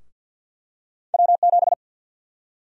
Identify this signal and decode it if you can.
Morse code, heard as a keyed tone, about 51 wpm, 700 Hz. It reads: W7
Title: USA Call Sign Prefixes - Repeated 2x - 50wpm